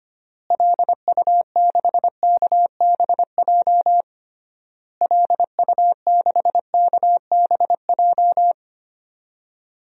Morse code, heard as a keyed tone, about 25 wpm, 690 Hz. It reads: LU6KBJ LU6KBJ
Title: DXCC-100 Call Signs - 6 Characters - Repeated 3x - 25wpm